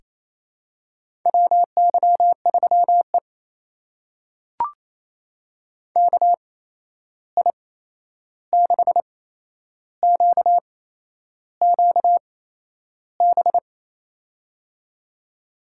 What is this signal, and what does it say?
Morse code, heard as a keyed tone, about 28 wpm, spaced as for 11 wpm, 700 Hz. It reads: WY3E KI6QQB
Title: USA Call Signs - 8x Character Spacing - 28wpm